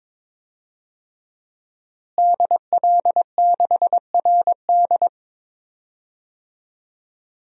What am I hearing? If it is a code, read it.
Morse code, heard as a keyed tone, about 22 wpm, 700 Hz. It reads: DL6RD
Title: DXCC-100 Call Signs - 5 Characters - Repeated 2x - 22wpm